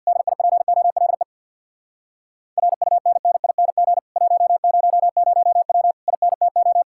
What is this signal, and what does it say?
Morse code, heard as a keyed tone, about 64 wpm, 700 Hz. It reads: DIPOLE RUNNING 100W INTO